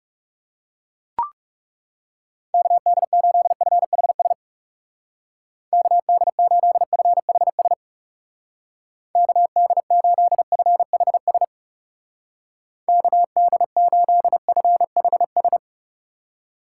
Morse code, heard as a keyed tone, about 37 wpm, 700 Hz. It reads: KD8FHS KD8FHS KD8FHS KD8FHS
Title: Speed Racing - USA Call Signs 30wpm